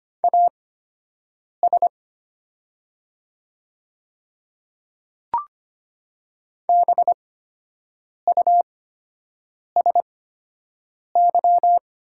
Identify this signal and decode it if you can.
Morse code, heard as a keyed tone, about 25 wpm, spaced as for 7 wpm, 700 Hz. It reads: AS BUSY
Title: 8x Character Spacing - Top 500 Words - 25wpm